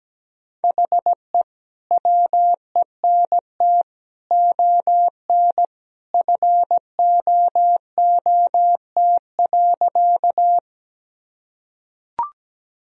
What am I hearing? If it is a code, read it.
Morse code, heard as a keyed tone, about 17 wpm, 700 Hz. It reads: HE WENT ON FOOT.
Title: Sentences from Top 1000 Words - Review Part 3 - 17wpm